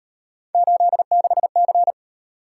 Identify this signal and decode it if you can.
Morse code, heard as a keyed tone, about 38 wpm, 700 Hz. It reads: 86C